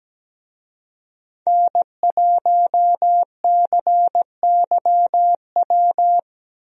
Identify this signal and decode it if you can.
Morse code, heard as a keyed tone, about 17 wpm, 700 Hz. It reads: N1CYW
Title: USA Call Signs - 5 Characters - 17wpm